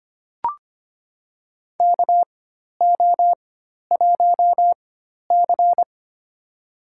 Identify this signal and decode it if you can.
Morse code, heard as a keyed tone, about 25 wpm, spaced as for 12 wpm, 700 Hz. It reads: KO1C
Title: USA Call Signs - 4x Character Spacing - 25wpm